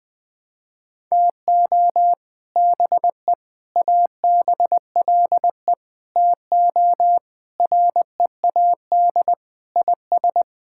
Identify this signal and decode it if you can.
Morse code, heard as a keyed tone, about 20 wpm, 700 Hz. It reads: TO BE ABLE TO READ IS
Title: Sentences with 3-Word Phrases - 20wpm